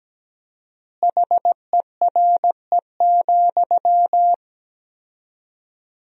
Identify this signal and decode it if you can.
Morse code, heard as a keyed tone, about 17 wpm, 700 Hz. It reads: HERE,
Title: 4x Word-Spacing - Sentences from Top 300 Words - 17wpm